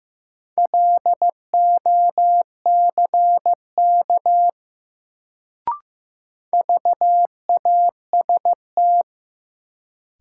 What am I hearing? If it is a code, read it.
Morse code, heard as a keyed tone, about 15 wpm, 700 Hz. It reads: LOCK VAST
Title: Sentences from Top 1300 Words - New Word Drill - 15wpm